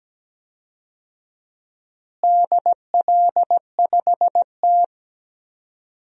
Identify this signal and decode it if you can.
Morse code, heard as a keyed tone, about 17 wpm, 700 Hz. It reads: DL5T
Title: DXCC-100 Call Signs - 4 Characters - Repeated 3x - 17wpm